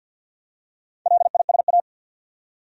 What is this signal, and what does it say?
Morse code, heard as a keyed tone, about 49 wpm, 700 Hz. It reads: LISA
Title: USA Common Names - Max 4 Letters - 50wpm